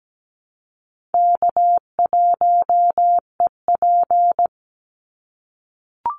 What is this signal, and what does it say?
Morse code, heard as a keyed tone, about 17 wpm, 700 Hz. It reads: K1EP
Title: Easy Call Signs - 17wpm